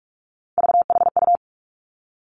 Morse code, heard as a keyed tone, about 46 wpm, 700 Hz. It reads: VHU
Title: Three Challenging Letter-Number Combinations 45wpm